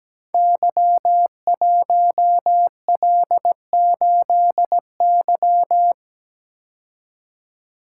Morse code, heard as a keyed tone, about 17 wpm, 690 Hz. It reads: Y1L8Y
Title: Code Groups - 5 Letter:Numbers - 17wpm